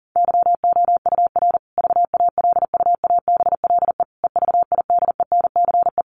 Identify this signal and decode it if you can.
Morse code, heard as a keyed tone, about 40 wpm, 700 Hz. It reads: YOUR VALUABLE EVIDENCE
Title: Sentences from Top 1300 Words 40wpm